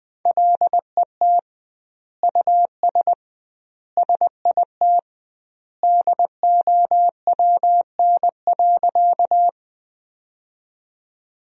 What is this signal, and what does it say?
Morse code, heard as a keyed tone, about 21 wpm, 700 Hz. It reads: LET US SIT DOWN.
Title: 2x Word-Spacing - Sentences from Top 500 Words - 20wpm